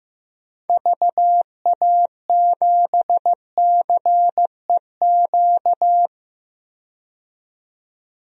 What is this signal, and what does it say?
Morse code, heard as a keyed tone, about 15 wpm, 700 Hz. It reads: VA7CEQ
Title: DXCC-100 Call Signs - 6 Characters - 15wpm